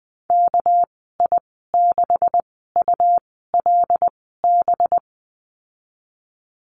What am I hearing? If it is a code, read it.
Morse code, heard as a keyed tone, about 20 wpm, spaced as for 15 wpm, 700 Hz. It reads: KI6ULB
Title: 200 Lesson 32g - Letter k - Call Signs 20wpm